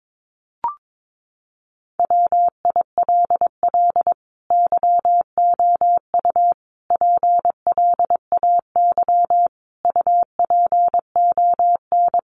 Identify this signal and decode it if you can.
Morse code, heard as a keyed tone, about 22 wpm, 700 Hz. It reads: WILL YOU PLAY UPON
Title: Sentences from Top 1500 Words 22wpm